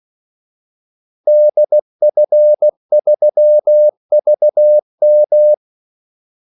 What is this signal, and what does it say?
Morse code, heard as a keyed tone, about 16 wpm, 600 Hz. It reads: DF3VM
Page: Kaldesignaler 16 wpm